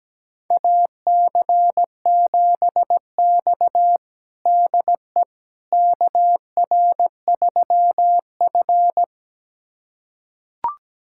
Morse code, heard as a keyed tone, about 17 wpm, 700 Hz. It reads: AC7X DE KR3F
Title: USA Callsign DE Callsign - 4 Characters - 17wpm